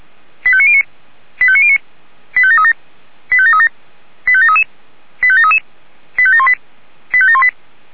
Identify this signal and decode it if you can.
Five-tone selective call (ZVEI), heard as quick runs of five tones, tones of about 70 ms. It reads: 85909 85909 85737 85737 85733 85733 85718 85718